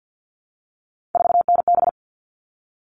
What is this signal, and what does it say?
Morse code, heard as a keyed tone, about 50 wpm, 700 Hz. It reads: 4NB
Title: All Cut Numbers - 3-Digit 50wpm